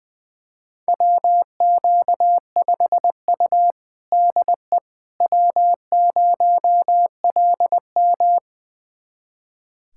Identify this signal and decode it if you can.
Morse code, heard as a keyed tone, about 20 wpm, 700 Hz. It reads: WQ5U DE W0LM
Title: USA Callsign DE Callsign - 4 Characters - 20wpm